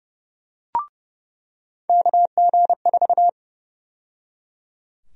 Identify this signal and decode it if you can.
Morse code, heard as a keyed tone, about 30 wpm, 700 Hz. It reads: KG4
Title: USA Call Sign Prefixes - 30wpm